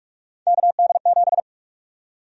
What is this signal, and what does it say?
Morse code, heard as a keyed tone, about 43 wpm, 700 Hz. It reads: KD7
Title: USA Call Sign Prefixes - Repeated 2x - 45wpm